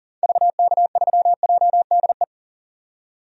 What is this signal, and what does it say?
Morse code, heard as a keyed tone, about 39 wpm, 700 Hz. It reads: VK3JDE
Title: DXCC-100 Call Signs - 6 Characters - 40wpm